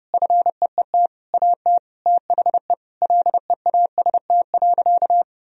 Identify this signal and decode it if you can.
Morse code, heard as a keyed tone, about 32 wpm, 700 Hz. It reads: FEET AT THE LEAST.
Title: Sentences from Top 1000 Words - Review Part 2 - 30wpm